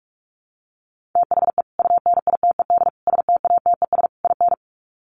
Morse code, heard as a keyed tone, about 44 wpm, 700 Hz. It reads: THE UNITED STATES IN
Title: Sets of 4 Words 45wpm